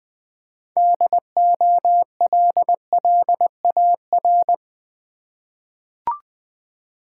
Morse code, heard as a keyed tone, about 21 wpm, 700 Hz. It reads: DOLLAR
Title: 6 Letter Words - First 3 Letters Given - 20wpm